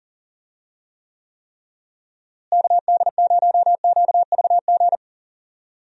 Morse code, heard as a keyed tone, about 40 wpm, 700 Hz. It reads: KD0QVG
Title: USA Call Signs - 6 Characters - Repeated 3x - 40wpm